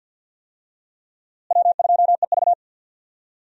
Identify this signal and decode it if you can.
Morse code, heard as a keyed tone, about 50 wpm, 700 Hz. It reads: W2EV
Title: USA Call Signs - 4 Characters - Repeated 2x - 50wpm